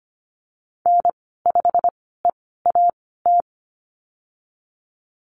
Morse code, heard as a keyed tone, about 25 wpm, spaced as for 16 wpm, 700 Hz. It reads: N5EAT
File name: Lesson 15g - Number 5 - Call Signs 25wpm